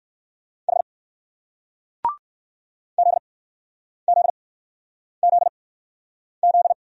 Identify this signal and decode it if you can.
Morse code, heard as a keyed tone, about 58 wpm, spaced as for 9 wpm, 700 Hz. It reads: H ZZZZ